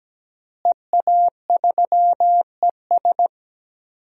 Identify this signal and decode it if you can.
Morse code, heard as a keyed tone, about 17 wpm, 700 Hz. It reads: EA3ES
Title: DXCC-100 Call Signs - 5 Characters - Repeated 2x - 17wpm